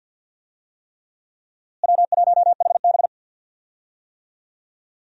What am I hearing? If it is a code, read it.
Morse code, heard as a keyed tone, about 53 wpm, 700 Hz. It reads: W1HB
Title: USA Call Signs - 4 Characters - Repeated 3x - 50wpm